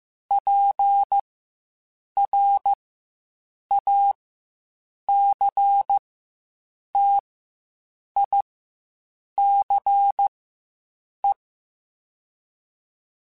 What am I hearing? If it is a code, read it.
Morse code, heard as a keyed tone, about 15 wpm, spaced as for 7 wpm, 800 Hz. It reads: PRACTICE
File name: Quotes Spaced / Quotes Spaced-mp3s / Deceive sp @10